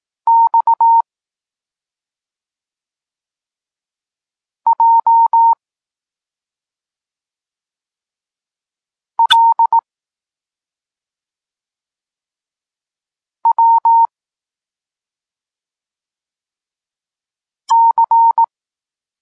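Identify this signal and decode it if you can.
Morse code, heard as a keyed tone, about 18 wpm, 920 Hz. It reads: X J L W C
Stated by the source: Polish Morse lesson teaching the letter C, from SP9MOA Niepołomicki Klub Krótkofalowców.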